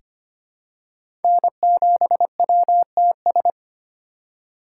Morse code, heard as a keyed tone, about 25 wpm, 700 Hz. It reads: N7WTS
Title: USA Call Signs - 8x Character Spacing - 25wpm